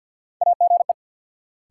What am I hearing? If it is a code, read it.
Morse code, heard as a keyed tone, about 48 wpm, 700 Hz. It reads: AGE
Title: Sets of 3 Words - New Word Drill - 50wpm